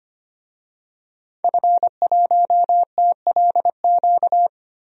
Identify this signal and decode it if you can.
Morse code, heard as a keyed tone, about 25 wpm, 700 Hz. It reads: F1TLQ
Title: DXCC-100 Call Signs - 5 Characters - 25wpm